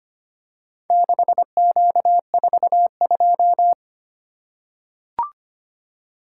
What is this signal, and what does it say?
Morse code, heard as a keyed tone, about 25 wpm, 700 Hz. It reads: 6Q42
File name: practice / 25wpm / Four Letter-Number Combinations - 25wpm